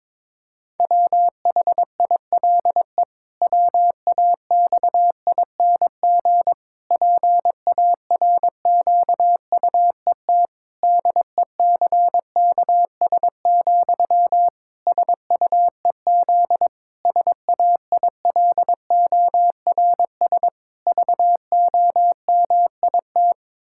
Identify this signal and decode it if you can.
Morse code, heard as a keyed tone, about 22 wpm, 700 Hz. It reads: WHILE WAXING PARQUET DECKS, SUEZ SAILORS VOMIT
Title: Pangram Challenge v2 - 22wpm